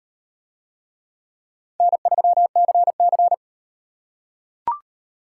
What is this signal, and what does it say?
Morse code, heard as a keyed tone, about 38 wpm, 700 Hz. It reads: N3CC